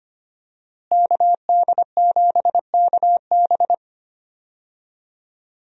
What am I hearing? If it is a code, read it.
Morse code, heard as a keyed tone, about 25 wpm, 700 Hz. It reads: KD7KB
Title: USA Call Signs - 5 Characters - Repeated 2x - 25wpm